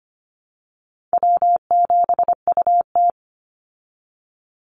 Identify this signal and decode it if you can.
Morse code, heard as a keyed tone, about 25 wpm, 700 Hz. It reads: W7UT
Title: Call Signs Repeated 3x - 25wpm